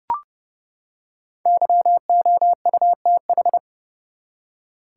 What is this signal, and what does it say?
Morse code, heard as a keyed tone, about 30 wpm, 700 Hz. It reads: YOUTH
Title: Five Letter Words - 30wpm